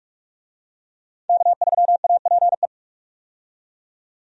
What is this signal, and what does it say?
Morse code, heard as a keyed tone, about 46 wpm, 700 Hz. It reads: K3APE
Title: USA Call Signs - 5 Characters - 45wpm